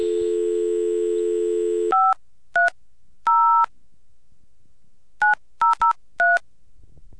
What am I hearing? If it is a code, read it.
DTMF, heard as telephone touch tones, tones of about 150 ms. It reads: 530 9003